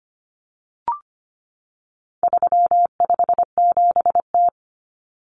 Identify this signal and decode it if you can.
Morse code, heard as a keyed tone, about 25 wpm, 700 Hz. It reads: 357T
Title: Common Cut Numbers - 4-Digit 25wpm